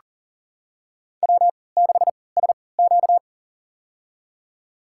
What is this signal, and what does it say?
Morse code, heard as a keyed tone, about 40 wpm, spaced as for 23 wpm, 700 Hz. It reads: W6SQ
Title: USA Call Signs - 4 Characters - 3x Character Spacing - 40wpm